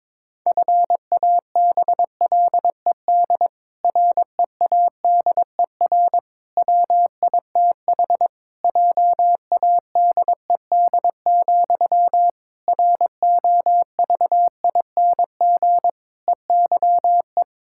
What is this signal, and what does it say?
Morse code, heard as a keyed tone, about 22 wpm, 700 Hz. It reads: FABLED READER WITH JADED, ROVING EYE
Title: Pangram Challenge v2 - 22wpm